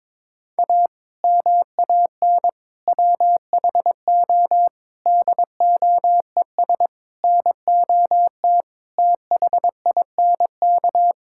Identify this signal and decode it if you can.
Morse code, heard as a keyed tone, about 22 wpm, 700 Hz. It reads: A MAN WHO DOES NOT THINK
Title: Sentences from Top 1600 Words 22wpm